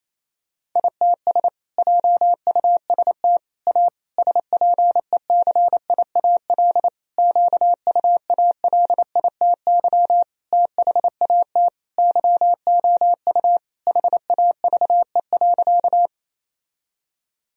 Morse code, heard as a keyed tone, about 28 wpm, 700 Hz. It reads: ITS JUST A SPECIAL QUALITY THAT YOU HAVE.
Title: Sentences from Top 1000 Words - Review Part 4 - 28wpm